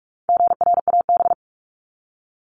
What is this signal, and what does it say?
Morse code, heard as a keyed tone, about 44 wpm, 700 Hz. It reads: GRAB
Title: Top 1751-2000 Words 45wpm